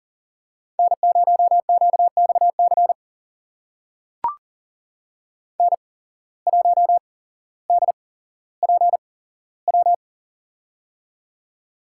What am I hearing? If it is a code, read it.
Morse code, heard as a keyed tone, about 40 wpm, spaced as for 17 wpm, 700 Hz. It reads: N0QXC N1DPW
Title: USA Call Signs - 5 Characters - 8x Character Spacing - 40wpm